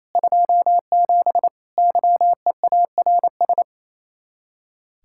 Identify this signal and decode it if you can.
Morse code, heard as a keyed tone, about 32 wpm, 700 Hz. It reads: 27 YEARS
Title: Speed-Racing Common QSO Phrases Encore - 28wpm